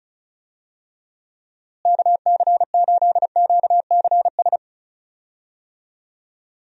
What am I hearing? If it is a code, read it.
Morse code, heard as a keyed tone, about 34 wpm, 700 Hz. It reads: KC8QCS
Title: USA Call Signs - 6 Characters - Repeated 3x - 35wpm